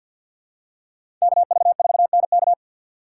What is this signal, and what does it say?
Morse code, heard as a keyed tone, about 52 wpm, 700 Hz. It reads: KV4NK
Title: USA Call Signs - 5 Characters - 3x Character Spacing - 50wpm